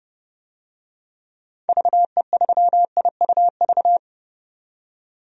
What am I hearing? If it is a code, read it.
Morse code, heard as a keyed tone, about 30 wpm, 700 Hz. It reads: VE3IUV
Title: DXCC-100 Call Signs - 6 Characters - 30wpm